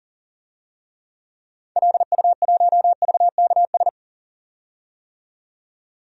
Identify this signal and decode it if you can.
Morse code, heard as a keyed tone, about 40 wpm, 700 Hz. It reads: LU1VKS
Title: DXCC-100 Call Signs - 6 Characters - 40wpm